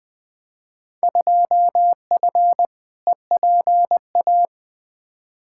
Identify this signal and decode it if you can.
Morse code, heard as a keyed tone, about 20 wpm, 700 Hz. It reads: 2F EPA
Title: Field Day Exchanges - 20wpm